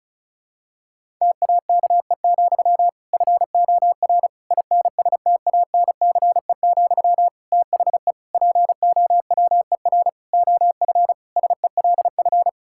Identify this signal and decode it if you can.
Morse code, heard as a keyed tone, about 36 wpm, 690 Hz. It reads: TAKE, FOR INSTANCE, THE POWER OF SELF
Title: Sentences from Top 1000 Words - Review Part 5 - 35wpm